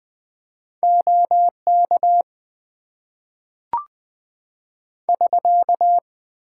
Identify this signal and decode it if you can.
Morse code, heard as a keyed tone, about 20 wpm, 700 Hz. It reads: O K <SK>
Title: Top 90 QSO Elements - 8x Character Spacing - 20wpm